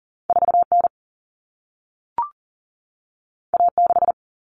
Morse code, heard as a keyed tone, about 40 wpm, 700 Hz. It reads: 4N A6
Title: Common Cut Numbers - 2-Digit 40wpm